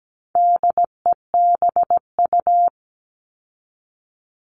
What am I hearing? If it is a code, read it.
Morse code, heard as a keyed tone, about 17 wpm, 700 Hz. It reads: DEBU
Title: All Cut Numbers - 4-Digit 17wpm